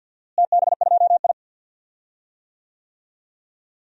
Morse code, heard as a keyed tone, about 46 wpm, 700 Hz. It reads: TB2I
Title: DXCC-100 Call Signs - 4 Characters - Repeated 3x - 50wpm